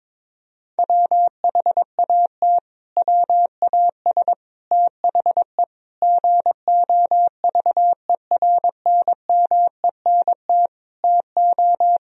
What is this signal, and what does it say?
Morse code, heard as a keyed tone, about 22 wpm, 700 Hz. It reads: WHAT WAS THE GOVERNMENT TO
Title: Sentences from Top 1000 Words - Review Part 3 - 22wpm